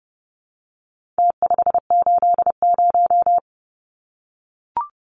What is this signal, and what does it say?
Morse code, heard as a keyed tone, about 30 wpm, 700 Hz. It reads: T580
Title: Numbers - Contest Style - 4-Digit 30wpm